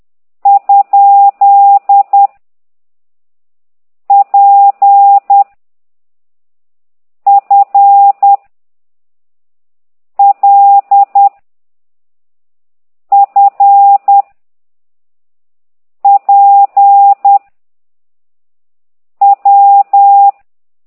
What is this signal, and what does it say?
Morse code, heard as a keyed tone, about 10 wpm, 800 Hz. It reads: ? P F L F P W
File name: ct2idw / exerc8